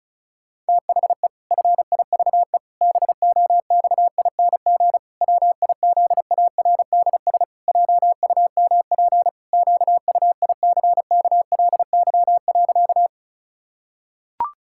Morse code, heard as a keyed tone, about 36 wpm, 700 Hz. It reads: THE FIVE BOXING WIZARDS JUMP QUICKLY.
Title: Pangram Challenge v2 - 35wpm